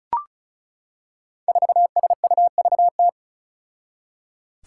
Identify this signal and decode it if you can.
Morse code, heard as a keyed tone, about 36 wpm, 700 Hz. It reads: 4SUVT
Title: Five Challenging Letter-Number Combinations 35wpm